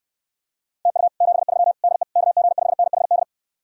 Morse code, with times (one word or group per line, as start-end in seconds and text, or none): 0.9-1.1 TU
1.2-1.7 73
1.8-2.0 DE
2.2-3.2 KG5NHD